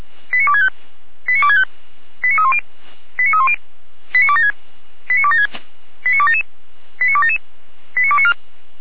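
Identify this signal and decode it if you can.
Five-tone selective call (ZVEI), heard as quick runs of five tones, tones of about 70 ms. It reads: 89265 89265 89310 89310 89276 89276 89277 89277 89294
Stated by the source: ZVEI five-tone selective call received on an SDR.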